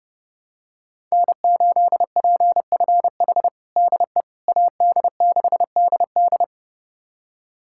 Morse code, heard as a keyed tone, about 30 wpm, 700 Hz. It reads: N8PFH DE AD6DD
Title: USA Callsign DE Callsign - 5 Characters - 30wpm